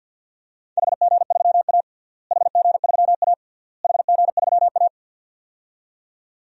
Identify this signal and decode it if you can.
Morse code, heard as a keyed tone, about 50 wpm, 700 Hz. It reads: HG3A HG3A HG3A
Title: DXCC-100 Call Signs - 4 Characters - Repeated 3x - 50wpm